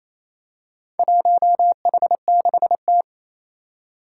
Morse code, heard as a keyed tone, about 27 wpm, 700 Hz. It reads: 1H6T
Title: Four Challenging Letter-Number Combinations 28wpm